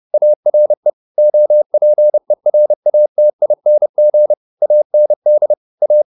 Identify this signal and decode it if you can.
Morse code, heard as a keyed tone, about 30 wpm, 600 Hz. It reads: ARE OPERATING AND A